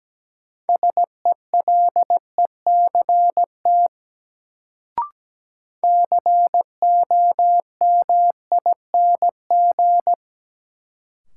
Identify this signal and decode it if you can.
Morse code, heard as a keyed tone, about 17 wpm, 700 Hz. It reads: SELECT COMING
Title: Sentences from Top 1300 Words - New Word Drill - 17wpm